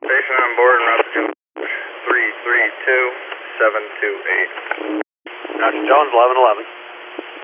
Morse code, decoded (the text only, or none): none